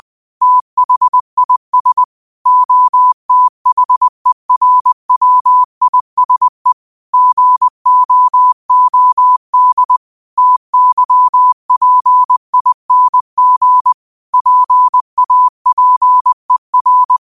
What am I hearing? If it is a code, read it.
Morse code, heard as a keyed tone, about 20 wpm, 1000 Hz. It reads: THIS OTHERWISE GOOD TYPING PAPER